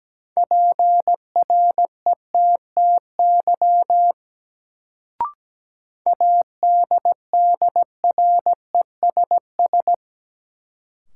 Morse code, heard as a keyed tone, about 17 wpm, 700 Hz. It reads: PRETTY ADDRESS
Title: Sentences from Top 1000 Words - New Word Drill - 17wpm